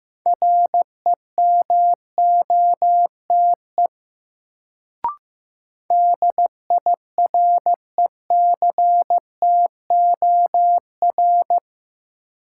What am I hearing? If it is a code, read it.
Morse code, heard as a keyed tone, about 15 wpm, 700 Hz. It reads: REMOTE DIRECTOR
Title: Sentences from Top 1300 Words - New Word Drill - 15wpm